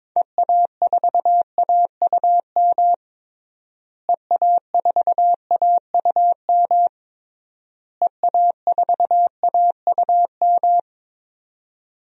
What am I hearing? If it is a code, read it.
Morse code, heard as a keyed tone, about 22 wpm, 700 Hz. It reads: EA4AUM EA4AUM EA4AUM
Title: DXCC-100 Call Signs - 6 Characters - Repeated 3x - 22wpm